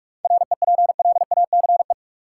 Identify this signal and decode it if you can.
Morse code, heard as a keyed tone, about 41 wpm, 700 Hz. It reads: REPLACE
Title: Seven Letter Words - 45wpm